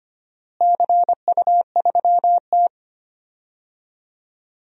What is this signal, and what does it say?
Morse code, heard as a keyed tone, about 25 wpm, 700 Hz. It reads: CU3T
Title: DXCC-100 Call Signs - 4 Characters - Repeated 3x - 25wpm